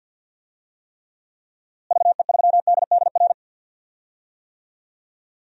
Morse code, heard as a keyed tone, about 52 wpm, 700 Hz. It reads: VE3DDR
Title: DXCC-100 Call Signs - 6 Characters - Repeated 3x - 50wpm